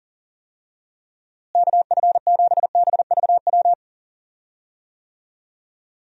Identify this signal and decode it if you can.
Morse code, heard as a keyed tone, about 40 wpm, 700 Hz. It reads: KF7BVW
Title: USA Call Signs - 6 Characters - Repeated 3x - 40wpm